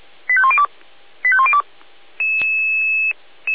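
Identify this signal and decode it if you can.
Five-tone selective call (ZVEI), heard as quick runs of five tones, tones of about 70 ms. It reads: 85102 85102
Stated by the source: Five-tone ZVEI selective call received with an SDR.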